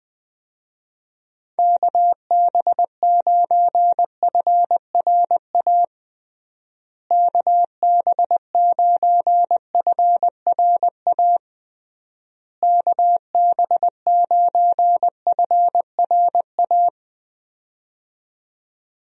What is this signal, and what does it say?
Morse code, heard as a keyed tone, about 20 wpm, 700 Hz. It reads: KB9FRA KB9FRA KB9FRA
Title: USA Call Signs - 6 Characters - Repeated 3x - 20wpm